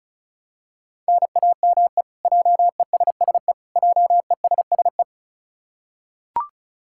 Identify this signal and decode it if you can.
Morse code, heard as a keyed tone, about 36 wpm, 700 Hz. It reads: NAME JESSE JESSE
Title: K1USN SST QSO Practice 35wpm